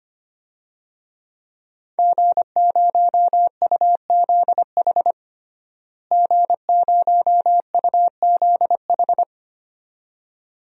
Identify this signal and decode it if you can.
Morse code, heard as a keyed tone, about 25 wpm, 700 Hz. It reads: G0UZH G0UZH
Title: DXCC-100 Call Signs - 5 Characters - Repeated 2x - 25wpm